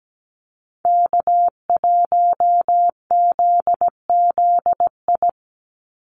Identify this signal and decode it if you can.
Morse code, heard as a keyed tone, about 17 wpm, 700 Hz. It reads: K1ZZI
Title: Call Signs - 17wpm